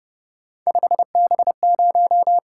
Morse code, heard as a keyed tone, about 29 wpm, 700 Hz. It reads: 5B0